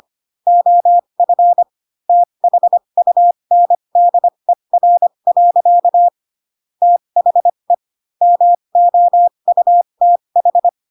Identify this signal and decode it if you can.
Morse code, heard as a keyed tone, about 25 wpm, spaced as for 22 wpm, 700 Hz. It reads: OF THUNDER. THE MOUTH
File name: Pinocchio / Pinocchio ch3 18 25 ik1hge